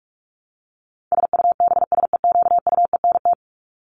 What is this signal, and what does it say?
Morse code, heard as a keyed tone, about 44 wpm, 700 Hz. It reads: SUBSEQUENT
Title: LY Suffix - Top 100 Words with Base Words 45wpm